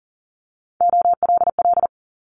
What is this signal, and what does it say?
Morse code, heard as a keyed tone, about 42 wpm, 700 Hz. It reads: OLL
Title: Multi-letter Phonemes Quick Fire 40wpm